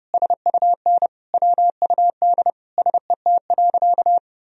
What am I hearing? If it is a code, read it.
Morse code, heard as a keyed tone, about 30 wpm, 700 Hz. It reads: SUN WUD SET.
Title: Sentences from Top 400 Words and Abbreviations 30wpm